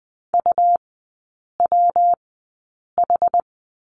Morse code, heard as a keyed tone, about 20 wpm, spaced as for 8 wpm, 700 Hz. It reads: UWH
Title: 184 Lesson 30e - Letter b - Three Characters 20wpm